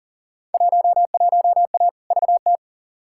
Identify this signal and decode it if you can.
Morse code, heard as a keyed tone, about 37 wpm, 700 Hz. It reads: 11A VT
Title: Field Day Exchanges Encore - 40wpm